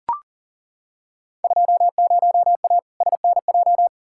Morse code, heard as a keyed tone, about 40 wpm, 700 Hz. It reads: 20A SNJ
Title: Field Day Exchanges Encore - 40wpm